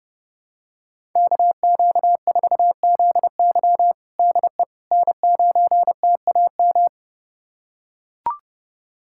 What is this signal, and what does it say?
Morse code, heard as a keyed tone, about 30 wpm, 700 Hz. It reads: KQ4ZY DE N9TAM
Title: USA Callsign DE Callsign - 5 Characters - 30wpm